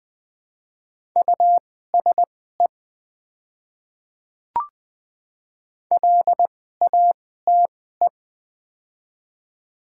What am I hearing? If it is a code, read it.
Morse code, heard as a keyed tone, about 20 wpm, spaced as for 14 wpm, 690 Hz. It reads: USE LATE
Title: 2x Character Spacing - Top 300 Words - 20wpm